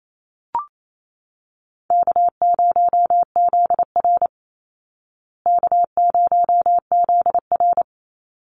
Morse code, heard as a keyed tone, about 28 wpm, 700 Hz. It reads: K0ZR K0ZR
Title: Easy Call Signs Repeated 2x - 28wpm